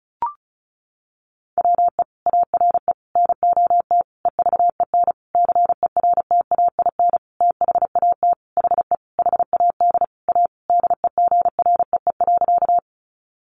Morse code, heard as a keyed tone, about 34 wpm, 700 Hz. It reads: WE ARE NOT EVEN CERTAIN THAT HE HAD A DEGREE.
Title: Sentences from Top 1500 Words 35wpm